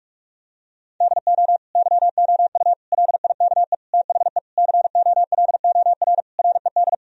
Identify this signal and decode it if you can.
Morse code, heard as a keyed tone, about 42 wpm, 700 Hz. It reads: DO YOU LIKE THE COLOR RED
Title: Sentences from Top 1000 Words - Review Part 2 - 45wpm